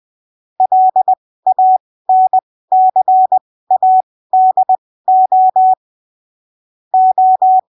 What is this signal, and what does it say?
Morse code, heard as a keyed tone, about 20 wpm, spaced as for 15 wpm, 750 Hz. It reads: LANCADO O
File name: arquivos / texto / texto 16 20ppm